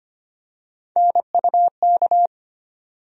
Morse code, heard as a keyed tone, about 25 wpm, 700 Hz. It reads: NUK
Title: USA Call Sign Postfixes - 3 Letters 25wpm